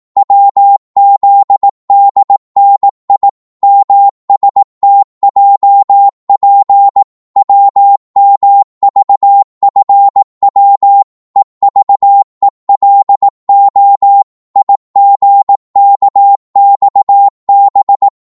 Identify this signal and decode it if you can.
Morse code, heard as a keyed tone, about 18 wpm, 800 Hz. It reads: WZDNI MSTJP WMVFW EVELO IGKXB
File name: Morse / ltr-18wpm